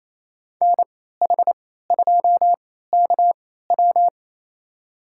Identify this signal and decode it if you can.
Morse code, heard as a keyed tone, about 27 wpm, spaced as for 16 wpm, 700 Hz. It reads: NH2KW
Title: USA Call Signs - 3x Character Spacing - 28wpm